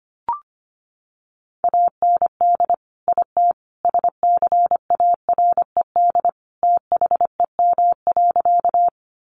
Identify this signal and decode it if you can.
Morse code, heard as a keyed tone, about 25 wpm, 700 Hz. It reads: AND IT SCARED THEM.
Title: Sentences from Top 700 Words 25wpm